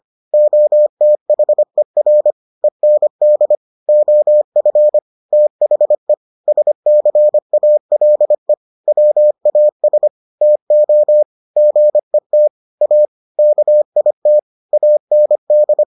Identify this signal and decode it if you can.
Morse code, heard as a keyed tone, about 25 wpm, 600 Hz. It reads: OTHER END OF THE SCALE WAS TO GET A KIT AND